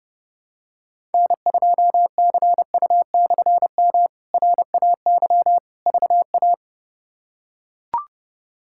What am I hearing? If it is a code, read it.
Morse code, heard as a keyed tone, about 29 wpm, 700 Hz. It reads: N2CU/M RAY VA
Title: K1USN SST QSO Practice - 30wpm